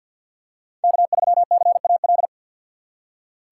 Morse code, heard as a keyed tone, about 52 wpm, 700 Hz. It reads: K3CAL
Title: USA Call Signs - 3x Character Spacing - 50wpm